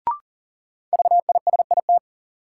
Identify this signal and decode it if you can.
Morse code, heard as a keyed tone, about 40 wpm, 700 Hz. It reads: VISIT
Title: Sentences from Top 900 Words - New Word Drill - 40wpm